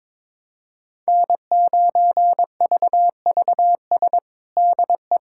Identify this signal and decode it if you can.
Morse code, heard as a keyed tone, about 22 wpm, 700 Hz. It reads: N9VVS DE
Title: USA Callsign DE Callsign 22wpm